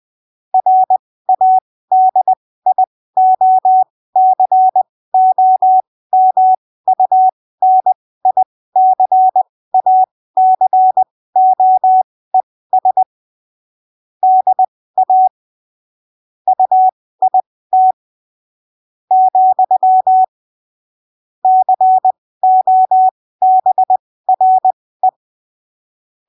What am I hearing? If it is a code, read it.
Morse code, heard as a keyed tone, about 20 wpm, spaced as for 15 wpm, 750 Hz. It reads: RADIOCOMUNICACOES DA UIT , COBRE
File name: arquivos / texto / texto 12 20ppm